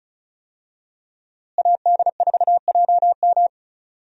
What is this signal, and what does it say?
Morse code, heard as a keyed tone, about 34 wpm, 700 Hz. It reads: AD4JM
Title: USA Call Signs - 5 Characters - Repeated 3x - 35wpm